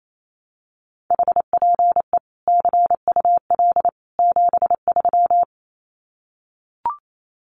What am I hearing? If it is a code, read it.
Morse code, heard as a keyed tone, about 28 wpm, 700 Hz. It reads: HPE CUL 73
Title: Common QSO Phrases 28wpm